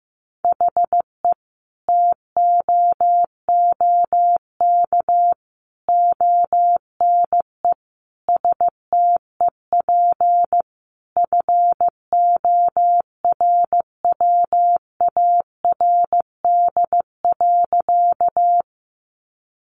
Morse code, heard as a keyed tone, about 14 wpm, 700 Hz. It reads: HE TOOK ONE STEP FORWARD.
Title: Sentences from Top 1300 Words 15wpm